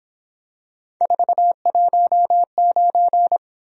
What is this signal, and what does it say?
Morse code, heard as a keyed tone, about 26 wpm, 700 Hz. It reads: 419
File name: Speed-Racing - Three Letter-Number Combinations - 22wpm